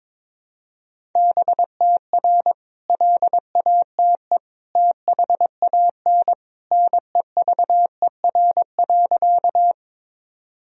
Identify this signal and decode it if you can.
Morse code, heard as a keyed tone, about 23 wpm, 700 Hz. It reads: BTR LATE THAN NEVER.
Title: Sentences from Top 400 Words and Abbreviations 22wpm